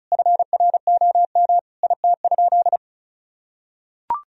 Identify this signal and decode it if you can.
Morse code, heard as a keyed tone, about 36 wpm, 700 Hz. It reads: FROM IT?
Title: Sentences from Top 1000 Words - Review Part 4 - 35wpm